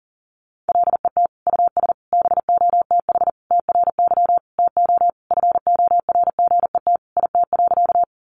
Morse code, heard as a keyed tone, about 40 wpm, 700 Hz. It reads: LET US BOTH TRY TO FORGET IT.
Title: Sentences from Top 1200 Words 40wpm